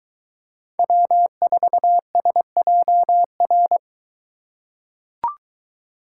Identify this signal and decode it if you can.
Morse code, heard as a keyed tone, about 21 wpm, 700 Hz. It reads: W4SJR